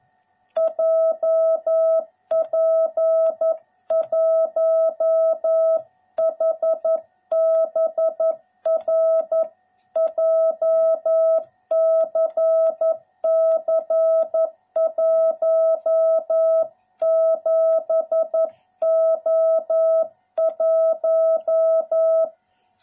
Morse code, heard as a keyed tone, about 11 wpm, 660 Hz. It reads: JP1HBR JCC 17O1